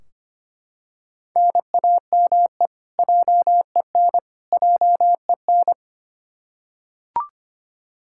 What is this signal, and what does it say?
Morse code, heard as a keyed tone, about 25 wpm, 700 Hz. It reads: NAME JEN JEN
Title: K1USN SST QSO Practice 25wpm